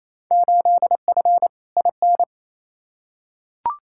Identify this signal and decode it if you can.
Morse code, heard as a keyed tone, about 28 wpm, 700 Hz. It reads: 8F IN
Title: Field Day Exchanges Encore - 28wpm